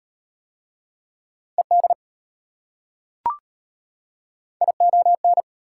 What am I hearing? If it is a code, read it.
Morse code, heard as a keyed tone, about 32 wpm, 700 Hz. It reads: ED ION